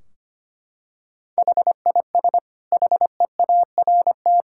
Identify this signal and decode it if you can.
Morse code, heard as a keyed tone, about 25 wpm, 700 Hz. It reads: HIS HEART
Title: Sentences from Top 1600 Words 25wpm